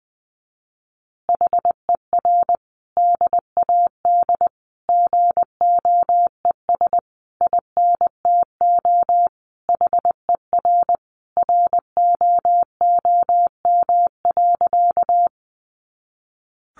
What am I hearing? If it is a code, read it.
Morse code, heard as a keyed tone, about 20 wpm, 700 Hz. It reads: HER DAD GOES INTO HER ROOM.
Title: Additional Easy Phrases 20wpm